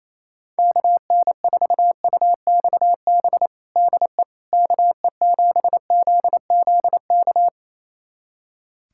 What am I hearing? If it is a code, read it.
Morse code, heard as a keyed tone, about 27 wpm, 700 Hz. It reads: KN4UXB DE KE7ZZK
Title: USA Callsign DE Callsign - 6 Characters - 28wpm